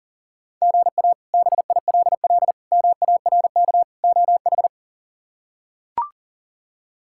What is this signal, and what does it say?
Morse code, heard as a keyed tone, about 40 wpm, 700 Hz. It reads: GA BILL MARK OH
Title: K1USN SST QSO Practice - 40wpm